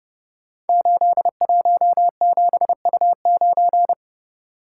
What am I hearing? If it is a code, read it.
Morse code, heard as a keyed tone, about 30 wpm, 700 Hz. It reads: 817U9
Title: Code Groups - 5 Letter:Numbers - 30wpm